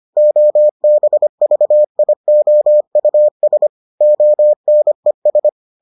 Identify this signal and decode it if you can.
Morse code, heard as a keyed tone, about 25 wpm, 600 Hz. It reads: OBVIOUS ONES